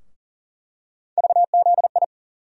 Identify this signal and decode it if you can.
Morse code, heard as a keyed tone, about 40 wpm, 700 Hz. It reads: VZI
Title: USA Call Sign Postfixes - 3 Letters 40wpm